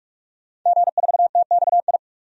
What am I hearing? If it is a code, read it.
Morse code, heard as a keyed tone, about 43 wpm, 700 Hz. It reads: G4TXI
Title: DXCC-100 Call Signs - 5 Characters - 45wpm